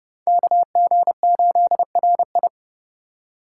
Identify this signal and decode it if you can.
Morse code, heard as a keyed tone, about 24 wpm, 700 Hz. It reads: KG8RI
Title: Speed Racing - USA Call Signs 20wpm